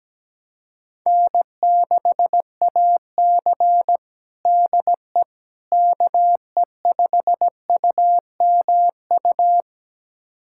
Morse code, heard as a keyed tone, about 17 wpm, 700 Hz. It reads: N6AC DE KE5UMU
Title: USA Callsign DE Callsign 17wpm